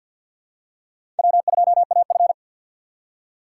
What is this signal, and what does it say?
Morse code, heard as a keyed tone, about 51 wpm, 700 Hz. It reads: W2AF
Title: USA Call Signs - 4 Characters - 3x Character Spacing - 50wpm